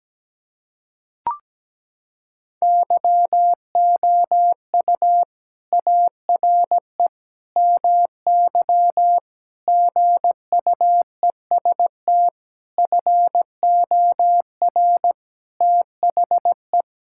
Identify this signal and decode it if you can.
Morse code, heard as a keyed tone, about 17 wpm, 700 Hz. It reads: YOU ARE MY GUEST FOR THE
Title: Sentences from Top 1600 Words 17wpm